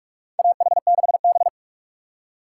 Morse code, heard as a keyed tone, about 44 wpm, 700 Hz. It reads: AH6B
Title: USA Call Signs - 4 Characters - 4x Character Spacing - 45wpm